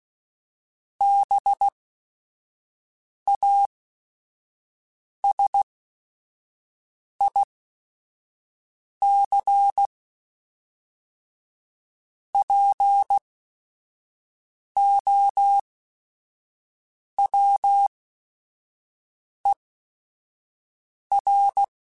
Morse code, heard as a keyed tone, about 16 wpm, spaced as for 5 wpm, 800 Hz. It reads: BASIC POWER
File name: Technical Subjects / Technical Subjects-mp3s / Sw Pwr Sup @5